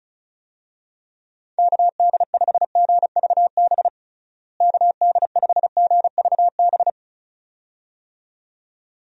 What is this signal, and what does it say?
Morse code, heard as a keyed tone, about 36 wpm, 700 Hz. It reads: KD5GVB KD5GVB
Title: USA Call Signs - 6 Characters - Repeated 2x - 35wpm